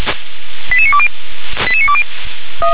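Five-tone selective call (ZVEI), heard as quick runs of five tones, tones of about 70 ms. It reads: 88920 88920
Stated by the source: ZVEI five-tone selective call received with an SDR.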